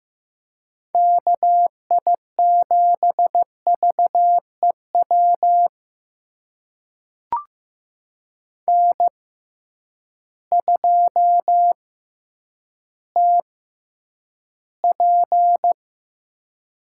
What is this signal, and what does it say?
Morse code, heard as a keyed tone, about 15 wpm, spaced as for 9 wpm, 700 Hz. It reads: KI7VEW N2TP
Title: USA Call Signs - 6x Character Spacing - 15wpm